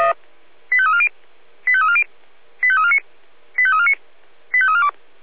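Five-tone selective call (ZVEI), heard as quick runs of five tones, tones of about 70 ms. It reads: 85340 85340 85349 85349 85342
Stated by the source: ZVEI five-tone selective call received via SDR.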